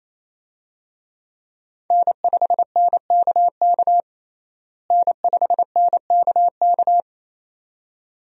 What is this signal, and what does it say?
Morse code, heard as a keyed tone, about 28 wpm, 700 Hz. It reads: N5NKK N5NKK
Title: USA Call Signs - 5 Characters - Repeated 2x - 28wpm